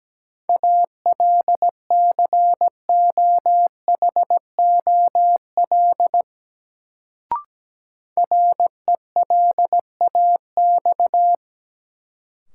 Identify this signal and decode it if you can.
Morse code, heard as a keyed tone, about 17 wpm, 700 Hz. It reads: ALCOHOL RELAX
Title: Sentences from Top 1400 Words - New Word Drill Alt - 17wpm